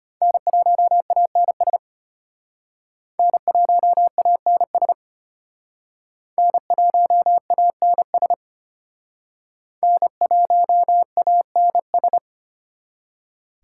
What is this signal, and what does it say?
Morse code, heard as a keyed tone, about 34 wpm, 700 Hz. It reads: N1ANS N1ANS N1ANS N1ANS